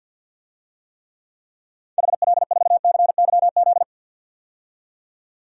Morse code, heard as a keyed tone, about 49 wpm, 700 Hz. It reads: HL4CYZ
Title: DXCC-100 Call Signs - 6 Characters - 50wpm